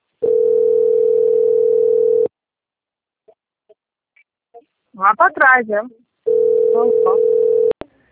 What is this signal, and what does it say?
Ringback tone (North America), 440 + 480 Hz, with speech.
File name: out- - - - - .7620